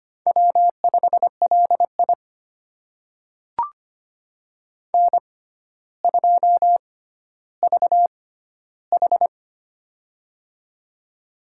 Morse code, heard as a keyed tone, about 25 wpm, spaced as for 13 wpm, 700 Hz. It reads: W5LI N2VH
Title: USA Call Signs - 4 Characters - 6x Character Spacing - 25wpm